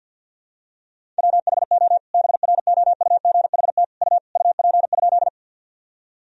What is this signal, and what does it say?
Morse code, heard as a keyed tone, about 51 wpm, 700 Hz. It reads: WHO BROUGHT U UP?
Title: Sentences from Top 500 Words and Abbreviations 50wpm